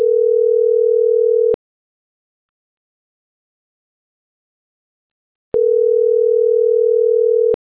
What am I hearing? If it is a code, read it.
Ringback tone (North America), 440 + 480 Hz.